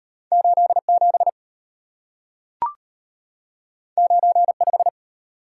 Morse code, heard as a keyed tone, about 34 wpm, 700 Hz. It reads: 87 95